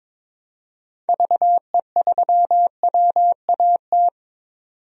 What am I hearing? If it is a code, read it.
Morse code, heard as a keyed tone, about 21 wpm, 700 Hz. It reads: VE3WAT
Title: DXCC-100 Call Signs - 6 Characters - 22wpm